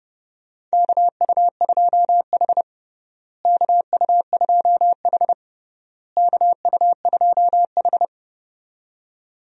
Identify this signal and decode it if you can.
Morse code, heard as a keyed tone, about 30 wpm, 700 Hz. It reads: KU2H KU2H KU2H
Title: USA Call Signs - 4 Characters - Repeated 3x - 30wpm